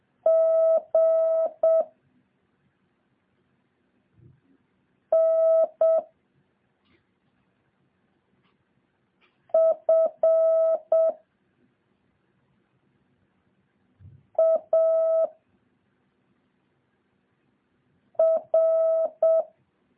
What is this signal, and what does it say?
Morse code, heard as a keyed tone, about 7 wpm, 650 Hz. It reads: G N F A R
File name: jh8cbh / 2023cw